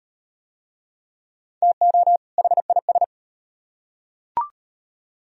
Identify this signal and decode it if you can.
Morse code, heard as a keyed tone, about 38 wpm, 700 Hz. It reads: TO HIS